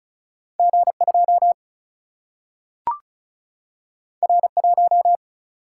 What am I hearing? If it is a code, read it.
Morse code, heard as a keyed tone, about 36 wpm, 700 Hz. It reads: G2 R1
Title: Single-Letter Single-Number 35wpm